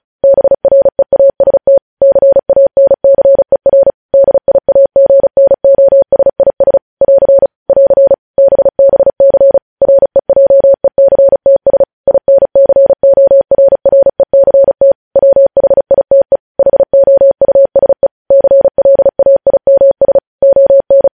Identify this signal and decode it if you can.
Morse code, heard as a keyed tone, about 36 wpm, 550 Hz. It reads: BREAST CANCER DIAGNOSIS <AR> <AR> BBC REJECTS INCORRECT WHITE HOUSE CLAIMS ON
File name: podcast / News headlines-1x-35x35